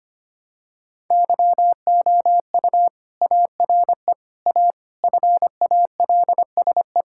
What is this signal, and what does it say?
Morse code, heard as a keyed tone, about 25 wpm, 700 Hz. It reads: YOU ARE A FALSE
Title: Sentences from Top 1600 Words 25wpm